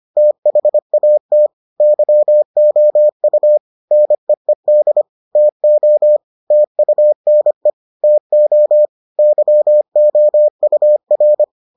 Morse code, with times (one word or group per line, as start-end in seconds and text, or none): 0.2-1.5 THAT
1.8-3.6 YOU
3.9-5.0 NEED
5.4-6.2 TO
6.5-7.7 TUNE
8.0-8.9 TO
9.2-11.4 YOUR